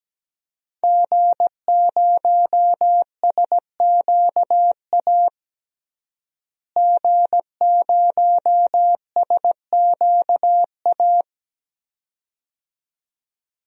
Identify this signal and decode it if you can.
Morse code, heard as a keyed tone, about 17 wpm, 700 Hz. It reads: G0SQA G0SQA
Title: DXCC-100 Call Signs - 5 Characters - Repeated 2x - 17wpm